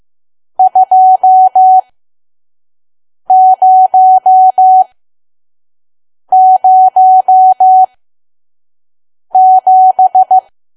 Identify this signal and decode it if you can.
Morse code, heard as a keyed tone, about 15 wpm, spaced as for 5 wpm, 750 Hz. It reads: 2007
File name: Archive / 5 WPM files / PM